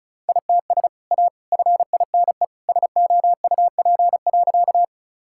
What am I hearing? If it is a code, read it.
Morse code, heard as a keyed tone, about 36 wpm, 690 Hz. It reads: ITS A FINE SOUP.
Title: Sentences from Top 1000 Words - Review Part 6 - 35wpm